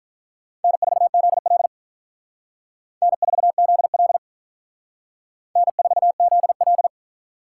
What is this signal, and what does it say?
Morse code, heard as a keyed tone, about 45 wpm, 700 Hz. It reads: N4ZL N4ZL N4ZL